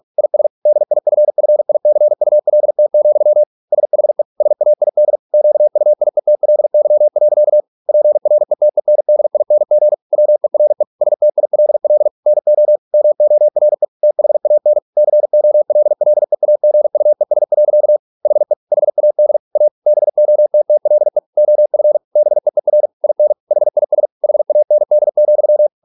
Morse code, heard as a keyed tone, about 46 wpm, 600 Hz. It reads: IS DIFFICULT, SHE SAID QUIETLY. PRETENDING WERE STILL NO MORE THAN COLLEAGUES. HE HAD A BOTTLE OF BEER IN HIS HAND,